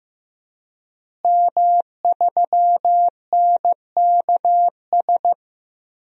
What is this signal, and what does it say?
Morse code, heard as a keyed tone, about 15 wpm, 700 Hz. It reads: M3NKS
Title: DXCC-100 Call Signs - 5 Characters - Repeated 3x - 15wpm